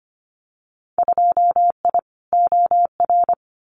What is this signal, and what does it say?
Morse code, heard as a keyed tone, about 25 wpm, 700 Hz. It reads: 2I OR
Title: Winter Field Day Exchanges Encore 25wpm